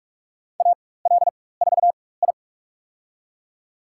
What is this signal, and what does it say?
Morse code, heard as a keyed tone, about 46 wpm, spaced as for 21 wpm, 700 Hz. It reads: AL4I
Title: USA Call Signs - 4 Characters - 4x Character Spacing - 45wpm